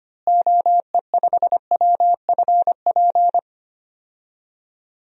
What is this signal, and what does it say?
Morse code, heard as a keyed tone, about 25 wpm, 700 Hz. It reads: OE5WFP
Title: DXCC-100 Call Signs - 6 Characters - Repeated 2x - 25wpm